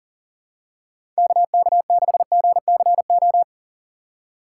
Morse code, heard as a keyed tone, about 40 wpm, 700 Hz. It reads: KK6GCO
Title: USA Call Signs - 6 Characters - 4x Character Spacing - 40wpm